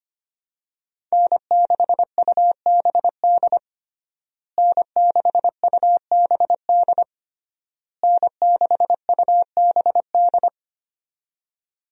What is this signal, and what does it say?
Morse code, heard as a keyed tone, about 25 wpm, 700 Hz. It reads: N6UBD N6UBD N6UBD
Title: USA Call Signs - 5 Characters - Repeated 3x - 25wpm